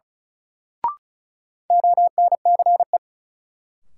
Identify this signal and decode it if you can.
Morse code, heard as a keyed tone, about 35 wpm, 700 Hz. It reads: ONCE
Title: Sentences from Top 300 Words - New Word Drill - 35wpm